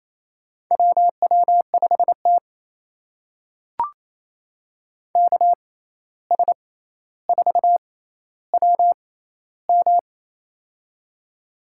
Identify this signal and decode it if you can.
Morse code, heard as a keyed tone, about 28 wpm, spaced as for 13 wpm, 700 Hz. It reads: WW5T KS4WM
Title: USA Call Signs - 6x Character Spacing - 28wpm